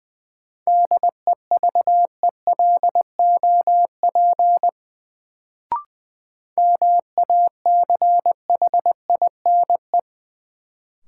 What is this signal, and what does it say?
Morse code, heard as a keyed tone, about 20 wpm, 700 Hz. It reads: DEVELOP MACHINE
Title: Sentences from Top 1000 Words - Review Part 2 - New Word Drill - 20wpm